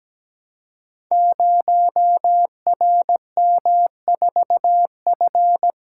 Morse code, heard as a keyed tone, about 17 wpm, 700 Hz. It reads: 0RM4F
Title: Code Groups - 5 Letter:Numbers - 17wpm